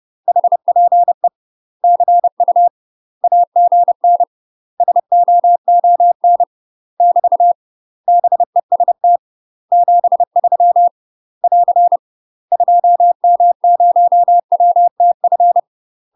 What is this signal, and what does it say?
Morse code, heard as a keyed tone, about 30 wpm, 700 Hz. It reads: HPE CU AGN SOON <BT> BEST 73 <AR> 2M0WTF
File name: UK MASTER QSOs TRAINING / 30WPM / UK MASTER QSOs 0049